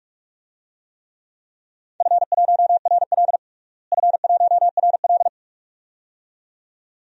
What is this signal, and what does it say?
Morse code, heard as a keyed tone, about 42 wpm, 700 Hz. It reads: F1RL F1RL
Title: DXCC-100 Call Signs - 4 Characters - Repeated 2x - 45wpm